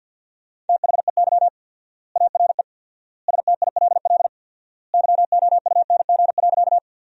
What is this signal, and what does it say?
Morse code, heard as a keyed tone, about 50 wpm, 700 Hz. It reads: THEY ARE STILL YOUNG.
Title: 4x Word-Spacing - Sentences from Top 400 Words - 50wpm